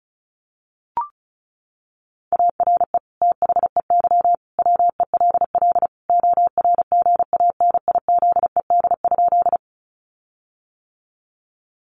Morse code, heard as a keyed tone, about 34 wpm, 700 Hz. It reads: ARE THEY WELL ORGANIZED?
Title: Sentences from Top 1300 Words 35wpm